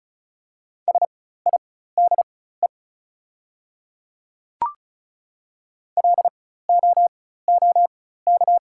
Morse code, heard as a keyed tone, about 36 wpm, spaced as for 16 wpm, 700 Hz. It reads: SIDE LOOK
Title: 4x Character Spacing - Top 100 Words - 35wpm